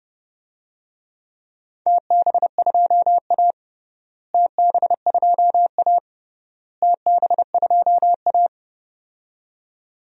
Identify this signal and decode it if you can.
Morse code, heard as a keyed tone, about 30 wpm, 700 Hz. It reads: TB2A TB2A TB2A
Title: DXCC-100 Call Signs - 4 Characters - Repeated 3x - 30wpm